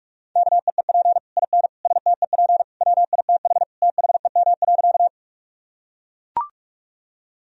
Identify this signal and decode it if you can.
Morse code, heard as a keyed tone, about 42 wpm, 700 Hz. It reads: KEEP IN STEP WITH THEM.
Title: Sentences from Top 1000 Words - Review Part 2 - 45wpm